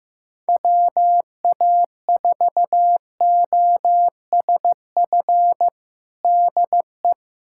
Morse code, heard as a keyed tone, about 15 wpm, 700 Hz. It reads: WA4OSF DE
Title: USA Callsign DE Callsign - 6 Characters - 15wpm